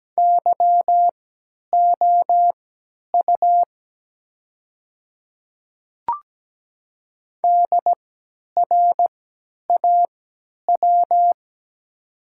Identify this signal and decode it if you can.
Morse code, heard as a keyed tone, about 17 wpm, spaced as for 10 wpm, 700 Hz. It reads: YOU DRAW
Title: 3x Character Spacing - Top 300 Words - 17wpm